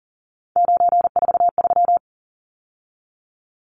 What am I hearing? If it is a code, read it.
Morse code, heard as a keyed tone, about 40 wpm, 700 Hz. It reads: 943
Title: Numbers - 3-Digit 40wpm